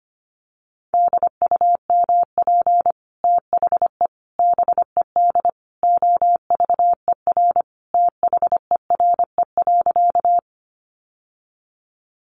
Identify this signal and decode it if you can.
Morse code, heard as a keyed tone, about 25 wpm, 700 Hz. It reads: DUMP THE BED OVER THERE.
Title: Sentences from Top 1300 Words 25wpm